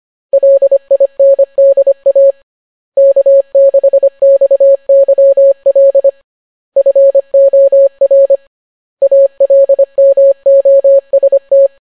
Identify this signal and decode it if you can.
Morse code, heard as a keyed tone, about 25 wpm, 550 Hz. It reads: LINDA K6XYL FOR ALMOST